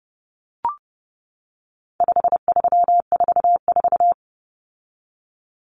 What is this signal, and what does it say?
Morse code, heard as a keyed tone, about 30 wpm, 700 Hz. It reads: 5344
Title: Common Cut Numbers - 4-Digit 30wpm